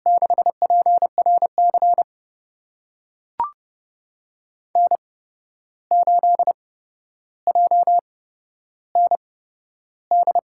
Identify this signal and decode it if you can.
Morse code, heard as a keyed tone, about 32 wpm, spaced as for 12 wpm, 700 Hz. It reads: 6PRC N8JND
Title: USA Call Signs - 5 Characters - 8x Character Spacing - 30wpm